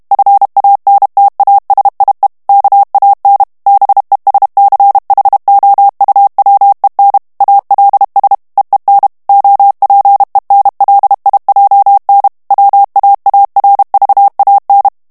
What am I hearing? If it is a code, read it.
Morse code, heard as a keyed tone, about 32 wpm, 800 Hz. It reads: FANTASIE KAN BESCHOUWEN ALS EEN OPEN LIJN WAARVAN
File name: HOLLANDAIS texte clair 1 vitesse 30 mots minute